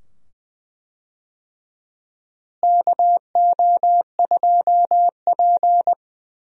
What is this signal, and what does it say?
Morse code, heard as a keyed tone, about 20 wpm, 700 Hz. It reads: KO2P
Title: USA Call Signs - 4 Characters - Repeated 2x - 20wpm